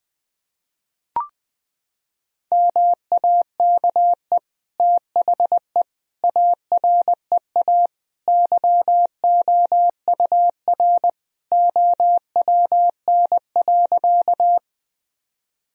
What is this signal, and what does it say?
Morse code, heard as a keyed tone, about 21 wpm, 700 Hz. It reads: MAKE THE AREA YOUR OWN.
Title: Sentences from Top 1000 Words - Review Part 2 - 20wpm